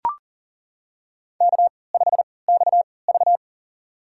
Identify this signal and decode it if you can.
Morse code, heard as a keyed tone, about 40 wpm, spaced as for 23 wpm, 700 Hz. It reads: K5XV
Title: USA Call Signs - 3x Character Spacing - 40wpm